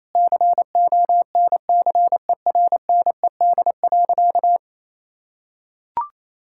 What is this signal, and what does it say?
Morse code, heard as a keyed tone, about 28 wpm, 700 Hz. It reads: CONCERNED.
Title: Sentences from Top 1000 Words - Review Part 6 - 28wpm